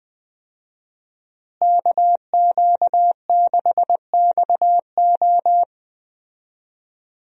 Morse code, heard as a keyed tone, about 20 wpm, 700 Hz. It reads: KQ6XO
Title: USA Call Signs - 5 Characters - Repeated 2x - 20wpm